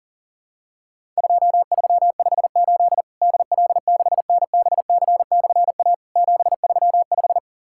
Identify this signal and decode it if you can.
Morse code, heard as a keyed tone, about 40 wpm, 700 Hz. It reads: 2358 DL6NBC/A 735
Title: Worked All Europe DX Contest - QTC Exchange - 40wpm